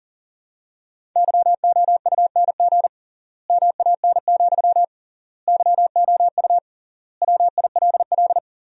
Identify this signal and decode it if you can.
Morse code, heard as a keyed tone, about 40 wpm, 700 Hz. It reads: YOUNG MAN, YOU WILL
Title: 3x Word-Spacing - Sentences from Top 400 Words - 40wpm